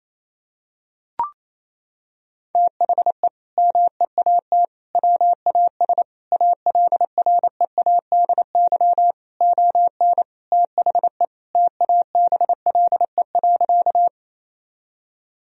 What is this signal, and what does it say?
Morse code, heard as a keyed tone, about 28 wpm, 700 Hz. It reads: THE MEAT WAS ALREADY ON THE TABLE.
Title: Sentences from Top 1000 Words - Review Part 3 - 28wpm